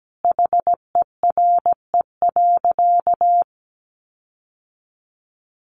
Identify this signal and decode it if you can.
Morse code, heard as a keyed tone, about 17 wpm, 700 Hz. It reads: HERE.
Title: Sentences from Top 1300 Words 17wpm